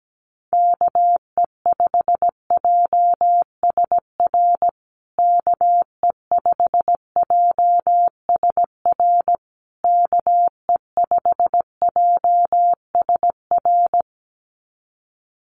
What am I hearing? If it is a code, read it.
Morse code, heard as a keyed tone, about 17 wpm, 700 Hz. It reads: KE5JSR KE5JSR KE5JSR
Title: USA Call Signs Repeated 3x 17wpm